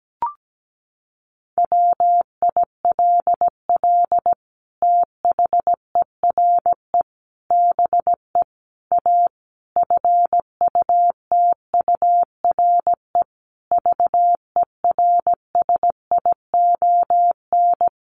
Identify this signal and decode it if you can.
Morse code, heard as a keyed tone, about 17 wpm, 700 Hz. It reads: WILL THERE BE A FUTURE VERSION
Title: Sentences from Top 1400 Words 17wpm